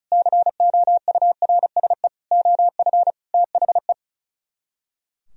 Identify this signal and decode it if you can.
Morse code, heard as a keyed tone, about 39 wpm, 700 Hz. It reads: COURSE OF THE
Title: Speed-Racing - Sets of 5 Words 35wpm